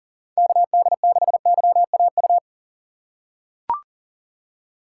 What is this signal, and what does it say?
Morse code, heard as a keyed tone, about 40 wpm, 700 Hz. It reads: KD6YAU
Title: USA Call Signs - 6 Characters - 2x Character Spacing - 40wpm